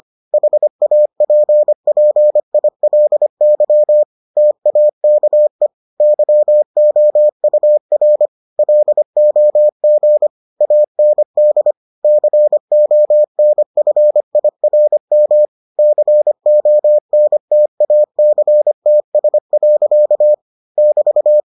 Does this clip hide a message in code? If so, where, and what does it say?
Yes: Morse code from 0.3 to 21.4 s, heard as a keyed tone, about 25 wpm, 600 Hz. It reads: HAPPILY TAKE YOUR LOG AND CONFIRM CONTACTS. <BT>